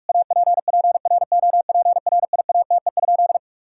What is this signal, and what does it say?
Morse code, heard as a keyed tone, about 46 wpm, 700 Hz. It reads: APPROPRIATE?
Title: Sentences from Top 1000 Words - Review Part 5 - 45wpm